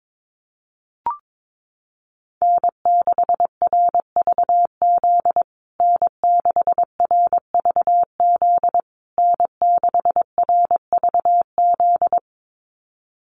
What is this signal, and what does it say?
Morse code, heard as a keyed tone, about 22 wpm, 700 Hz. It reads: N6RVZ N6RVZ N6RVZ
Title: USA Call Signs Repeated 3x 22wpm